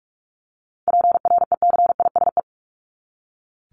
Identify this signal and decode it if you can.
Morse code, heard as a keyed tone, about 44 wpm, 700 Hz. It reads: PRECISE
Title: LY Suffix - Top 100 Words with Base Words 45wpm